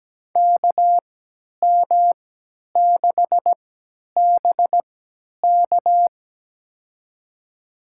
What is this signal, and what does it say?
Morse code, heard as a keyed tone, about 17 wpm, spaced as for 10 wpm, 700 Hz. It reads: KM6BK
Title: USA Call Signs - 5 Characters - 3x Character Spacing - 17wpm